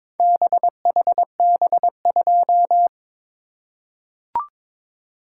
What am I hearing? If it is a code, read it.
Morse code, heard as a keyed tone, about 22 wpm, 700 Hz. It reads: BHB2
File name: practice / 22wpm / Four Letter-Number Combinations - 22wpm